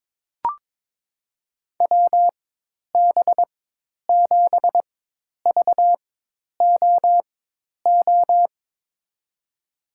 Morse code, heard as a keyed tone, about 22 wpm, spaced as for 10 wpm, 700 Hz. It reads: WB7VOO
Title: USA Call Signs - 6 Characters - 4x Character Spacing - 22wpm